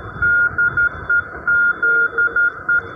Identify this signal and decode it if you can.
Morse code, heard as a keyed tone, about 11 wpm, 1420 Hz. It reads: DEZE